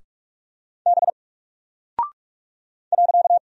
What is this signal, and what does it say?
Morse code, heard as a keyed tone, about 44 wpm, 700 Hz. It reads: B .
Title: Single Character (Letter:Number:Punctuation:Prosign) 45wpm